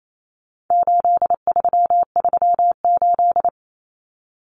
Morse code, heard as a keyed tone, about 28 wpm, 700 Hz. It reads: 8338
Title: Numbers - 4-Digit 28wpm